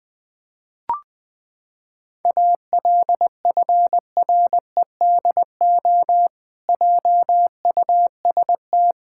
Morse code, heard as a keyed tone, about 20 wpm, 700 Hz. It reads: ALFREDO JUST
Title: Pangram Challenge v2 - 20wpm